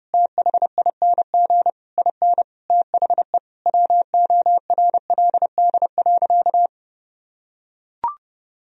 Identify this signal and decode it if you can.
Morse code, heard as a keyed tone, about 30 wpm, 700 Hz. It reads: THING IN THE WORLD.
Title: Sentences from Top 1000 Words - Review Part 3 - 30wpm